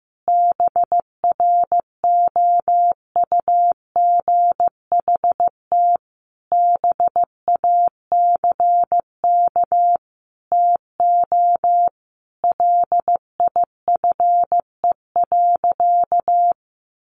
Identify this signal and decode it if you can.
Morse code, heard as a keyed tone, about 15 wpm, 700 Hz. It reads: BROUGHT BACK TO LIFE.
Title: Sentences from Top 600 Words 15wpm